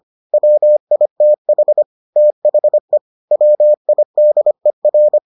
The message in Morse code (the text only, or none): WITH THE WIDER